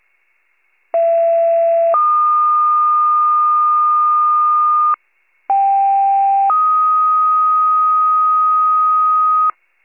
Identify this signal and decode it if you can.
Two-tone page, heard as a one-second tone, then a three-second tone, about 670 then 1220 Hz, about 790 then 1250 Hz. It is heard from 0.9 to 9.5 s.
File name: wcalert